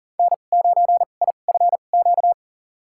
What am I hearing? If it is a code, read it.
Morse code, heard as a keyed tone, about 40 wpm, spaced as for 29 wpm, 700 Hz. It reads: N9IFQ
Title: USA Call Signs - 2x Character Spacing - 40wpm